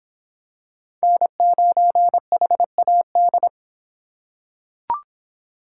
Morse code, heard as a keyed tone, about 26 wpm, 700 Hz. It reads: N9HAD